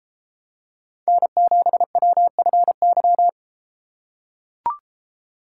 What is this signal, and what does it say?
Morse code, heard as a keyed tone, about 33 wpm, 700 Hz. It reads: N7WFY